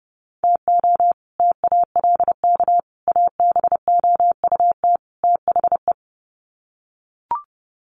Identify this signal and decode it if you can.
Morse code, heard as a keyed tone, about 30 wpm, 700 Hz. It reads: TO TALK ABOUT THE
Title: Sets of 4 Words 30wpm